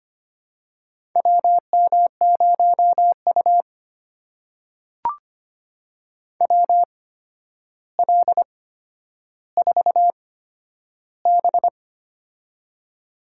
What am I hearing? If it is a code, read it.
Morse code, heard as a keyed tone, about 25 wpm, spaced as for 11 wpm, 700 Hz. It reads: WM0U WL4B
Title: USA Call Signs - 4 Characters - 8x Character Spacing - 25wpm